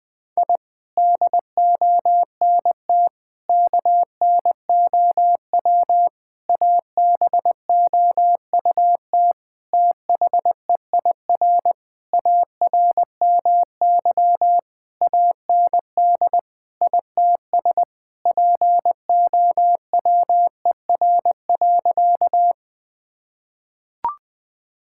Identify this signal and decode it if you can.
Morse code, heard as a keyed tone, about 20 wpm, 700 Hz. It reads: I DONT KNOW ABOUT THEIR ARMY AND ITS POWER.
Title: Sentences from Top 1000 Words - Review Part 5 - 20wpm